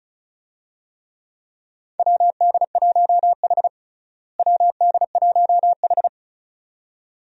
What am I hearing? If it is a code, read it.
Morse code, heard as a keyed tone, about 36 wpm, 700 Hz. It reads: WD1H WD1H
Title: USA Call Signs - 4 Characters - Repeated 2x - 35wpm